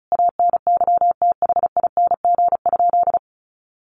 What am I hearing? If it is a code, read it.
Morse code, heard as a keyed tone, about 34 wpm, 700 Hz. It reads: ANYTHING?
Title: Sentences from Top 1100 Words 35wpm